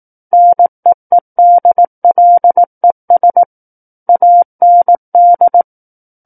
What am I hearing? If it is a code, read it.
Morse code, heard as a keyed tone, about 18 wpm, 700 Hz. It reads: NEEDLES AND
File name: Book One / 09-Chapter 9 - 18 WPM